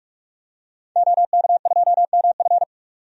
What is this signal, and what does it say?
Morse code, heard as a keyed tone, about 46 wpm, 700 Hz. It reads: OK2MF
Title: DXCC-100 Call Signs - 5 Characters - 45wpm